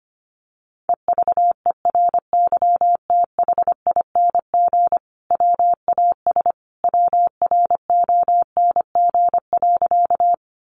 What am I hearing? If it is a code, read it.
Morse code, heard as a keyed tone, about 25 wpm, 700 Hz. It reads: EVERYTHING WAS WRONG.
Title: Sentences from Top 900 Words 25wpm